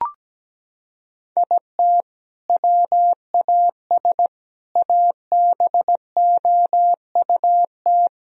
Morse code, heard as a keyed tone, about 17 wpm, 700 Hz. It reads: IT WAS ABOUT
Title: Sentences from Top 1000 Words - Review Part 1 - 17wpm